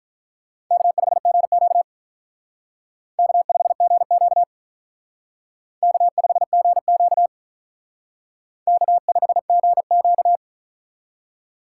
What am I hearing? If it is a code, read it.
Morse code, heard as a keyed tone, about 45 wpm, 700 Hz. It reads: K5GQ K5GQ K5GQ K5GQ